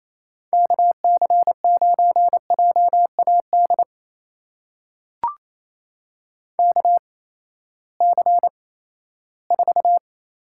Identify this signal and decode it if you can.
Morse code, heard as a keyed tone, about 28 wpm, spaced as for 16 wpm, 700 Hz. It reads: KC9JAD KC4
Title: USA Call Signs - 6 Characters - 8x Character Spacing - 28wpm